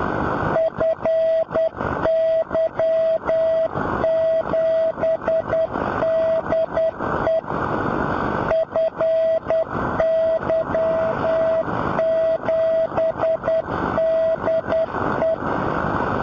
Morse code, heard as a keyed tone, about 10 wpm, 650 Hz, with radio noise. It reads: FY7DE FY7DE
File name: boos / fy7de-b-6m-beacon